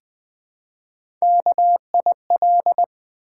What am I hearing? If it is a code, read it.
Morse code, heard as a keyed tone, about 20 wpm, 700 Hz. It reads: KIL
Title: Build A Word - 1 Letter At A Time - 4-Letter Words - 20wpm